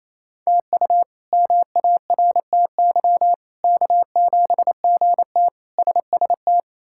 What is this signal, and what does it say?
Morse code, heard as a keyed tone, about 28 wpm, 700 Hz. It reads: TU MARTY K7GT SST
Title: K1USN SST QSO Practice - 28wpm